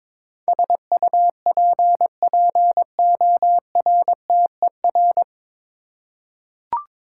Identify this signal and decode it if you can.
Morse code, heard as a keyed tone, about 21 wpm, 700 Hz. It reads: SUPPORTER
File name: 9 Letter Words - First 3 Letters Given - 22wpm